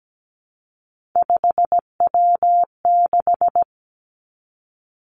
Morse code, heard as a keyed tone, about 17 wpm, 700 Hz. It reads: 5W6
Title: Three Challenging Letter-Number Combinations 17wpm